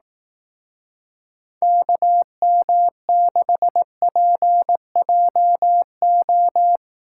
Morse code, heard as a keyed tone, about 18 wpm, 700 Hz. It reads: KM6PJO
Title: Speed Racing - USA Call Signs 15wpm